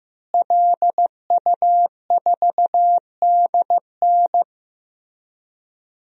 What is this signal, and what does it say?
Morse code, heard as a keyed tone, about 15 wpm, 700 Hz. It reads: LU4DN
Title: DXCC-100 Call Signs - 5 Characters - Repeated 3x - 15wpm